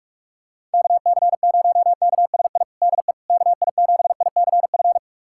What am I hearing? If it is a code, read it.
Morse code, heard as a keyed tone, about 46 wpm, 700 Hz. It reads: KC0KSI DE KI7ICF
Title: USA Callsign DE Callsign - 6 Characters - 45wpm